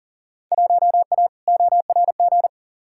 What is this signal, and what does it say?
Morse code, heard as a keyed tone, about 40 wpm, 700 Hz. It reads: 1A ORG
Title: Field Day Exchanges - 40wpm